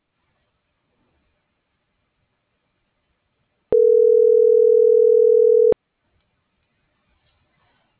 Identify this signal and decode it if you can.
Ringback tone (North America), 440 + 480 Hz.